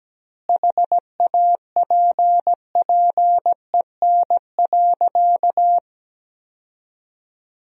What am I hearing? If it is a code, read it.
Morse code, heard as a keyed tone, about 17 wpm, 700 Hz. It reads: HAPPEN.
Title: 2x Word-Spacing - Sentences from Top 400 Words - 17wpm